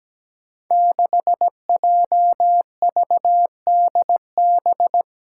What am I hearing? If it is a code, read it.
Morse code, heard as a keyed tone, about 17 wpm, 700 Hz. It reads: 6JVDB
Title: Five Challenging Letter-Number Combinations 17wpm